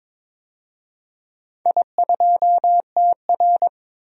Morse code, heard as a keyed tone, about 23 wpm, 700 Hz. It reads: I2TR
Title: DXCC-100 Call Signs - 4 Characters - 22wpm